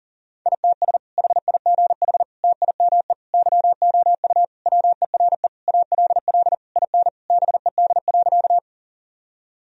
Morse code, heard as a keyed tone, about 37 wpm, 700 Hz. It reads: ITS HIGH TIME YOU WERE ALL IN BED.
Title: Sentences from Top 1000 Words - Review Part 1 - 40wpm